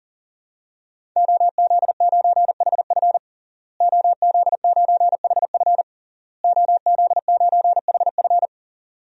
Morse code, heard as a keyed tone, about 40 wpm, 700 Hz. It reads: OZ9HF OZ9HF OZ9HF
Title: DXCC-100 Call Signs - 5 Characters - Repeated 3x - 40wpm